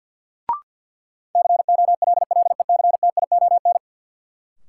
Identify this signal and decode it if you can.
Morse code, heard as a keyed tone, about 53 wpm, 700 Hz. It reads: COLLECTION
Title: Sentences from Top 1200 Words - New Word Drill Alt - 50wpm